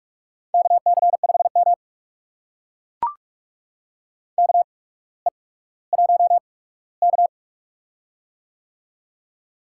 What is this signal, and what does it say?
Morse code, heard as a keyed tone, about 44 wpm, spaced as for 19 wpm, 700 Hz. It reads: KC5M KE1K
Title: USA Call Signs - 4 Characters - 8x Character Spacing - 45wpm